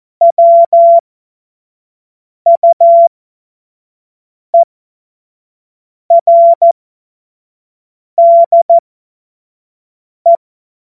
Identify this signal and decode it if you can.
Morse code, heard as a keyed tone, about 14 wpm, spaced as for 5 wpm, 680 Hz. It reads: WUERDE
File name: Witz-08-Juli-2025-25-F70